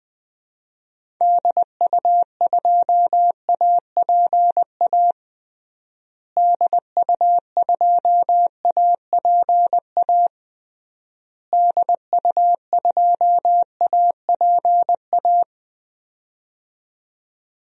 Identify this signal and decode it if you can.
Morse code, heard as a keyed tone, about 20 wpm, 700 Hz. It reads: DU2APA DU2APA DU2APA
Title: DXCC-100 Call Signs - 6 Characters - Repeated 3x - 20wpm